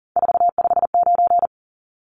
Morse code, heard as a keyed tone, about 40 wpm, 700 Hz. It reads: 459
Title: Numbers - 3-Digit 40wpm